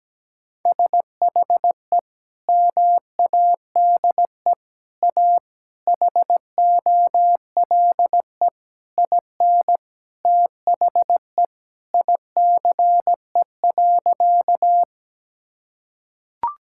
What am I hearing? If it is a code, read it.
Morse code, heard as a keyed tone, about 17 wpm, 700 Hz. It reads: SHE MADE A HOLE IN THE ICE.
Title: Sentences from Top 1600 Words 17wpm